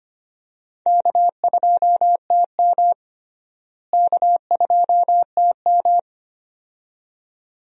Morse code, heard as a keyed tone, about 25 wpm, 700 Hz. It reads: K2TM K2TM
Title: USA Call Signs - 4 Characters - Repeated 2x - 25wpm